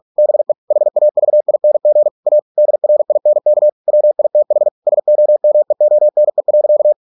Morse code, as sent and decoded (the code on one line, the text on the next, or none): -... / .... .- ...- .. -. --. / .- / -.. .-. .. -. -.- / .-- .. - .... / ... --- -- . --- -. .-.-.-
BE HAVING A DRINK WITH SOMEONE.